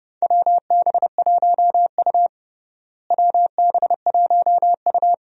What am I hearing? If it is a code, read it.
Morse code, heard as a keyed tone, about 30 wpm, 700 Hz. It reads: WB1U WB1U
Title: USA Call Signs - 4 Characters - Repeated 3x - 30wpm